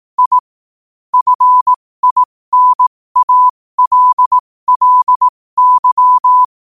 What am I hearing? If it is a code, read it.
Morse code, heard as a keyed tone, about 18 wpm, spaced as for 15 wpm, 1000 Hz. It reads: I FINALLY
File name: qotdfiles / 15wpm